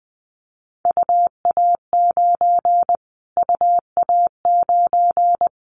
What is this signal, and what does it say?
Morse code, heard as a keyed tone, about 20 wpm, 700 Hz. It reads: UA9 UA9
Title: Callsign Prefixes for Top 25 DXCC - Repeated 3x - 20wpm